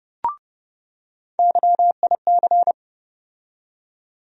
Morse code, heard as a keyed tone, about 30 wpm, 700 Hz. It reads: YIC
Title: USA Call Sign Postfixes - 3 Letters 30wpm